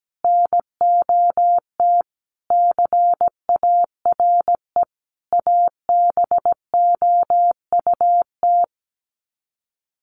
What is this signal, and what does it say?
Morse code, heard as a keyed tone, about 17 wpm, 700 Hz. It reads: NOT CARE ABOUT
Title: Sets of 5 Words Encore2 17wpm